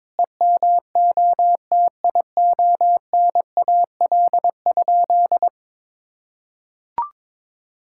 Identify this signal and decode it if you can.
Morse code, heard as a keyed tone, about 22 wpm, 700 Hz. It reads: EMOTIONAL?
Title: Sentences from Top 1000 Words - Review Part 5 - 22wpm